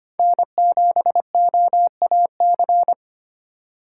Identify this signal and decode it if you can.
Morse code, heard as a keyed tone, about 25 wpm, 700 Hz. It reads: N7OAC
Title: USA Call Signs - 5 Characters - 3x Character Spacing - 25wpm